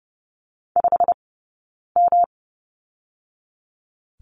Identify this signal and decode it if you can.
Morse code, heard as a keyed tone, about 30 wpm, spaced as for 9 wpm, 700 Hz. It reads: 5M
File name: Lesson 20d - Number 3 - Two Characters 30wpm